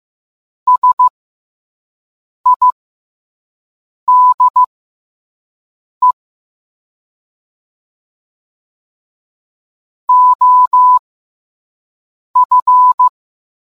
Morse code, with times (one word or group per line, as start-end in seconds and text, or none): 0.7-6.1 SIDE
10.1-13.1 OF